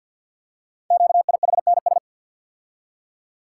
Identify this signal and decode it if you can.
Morse code, heard as a keyed tone, about 50 wpm, 700 Hz. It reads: QIHNS
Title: Code Groups - 5 Letters - 50wpm